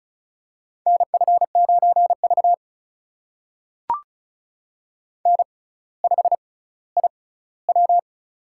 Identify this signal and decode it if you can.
Morse code, heard as a keyed tone, about 36 wpm, spaced as for 18 wpm, 700 Hz. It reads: NF9V N5IW
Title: USA Call Signs - 4 Characters - 6x Character Spacing - 35wpm